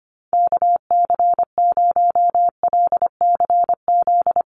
Morse code, heard as a keyed tone, about 25 wpm, 700 Hz. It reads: KC0LCZ
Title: USA Call Signs 25wpm